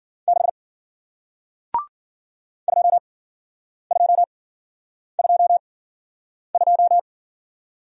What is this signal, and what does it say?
Morse code, heard as a keyed tone, about 55 wpm, spaced as for 9 wpm, 700 Hz. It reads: 6 2222